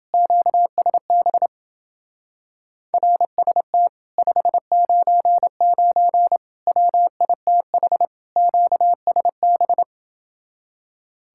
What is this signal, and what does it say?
Morse code, heard as a keyed tone, about 31 wpm, 700 Hz. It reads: QSB RST 599 WITH QSB